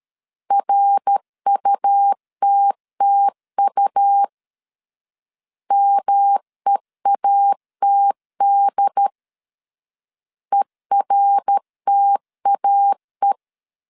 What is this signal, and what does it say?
Morse code, heard as a keyed tone, about 13 wpm, 800 Hz. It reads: RUTTU MEATD ERTAE
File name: CW / DET / 1 3 1500C S C 12wpm 2750